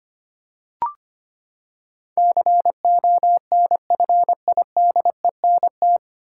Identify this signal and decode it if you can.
Morse code, heard as a keyed tone, about 25 wpm, 700 Hz. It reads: CONFIDENT
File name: Nine Letter Words - 25wpm